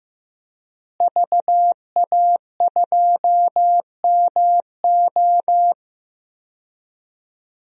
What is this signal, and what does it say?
Morse code, heard as a keyed tone, about 15 wpm, 700 Hz. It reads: VA2MO
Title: DXCC-100 Call Signs - 5 Characters - Repeated 3x - 15wpm